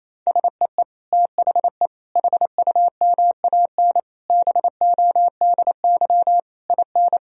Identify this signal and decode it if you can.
Morse code, heard as a keyed tone, about 28 wpm, 700 Hz. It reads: SEE THE HUMAN BODY IN
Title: Sentences from Top 1000 Words - Review Part 4 - 28wpm